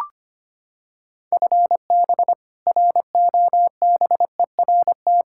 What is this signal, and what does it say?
Morse code, heard as a keyed tone, about 25 wpm, 700 Hz. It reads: FB ROBERT
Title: K1USN SST QSO Practice 25wpm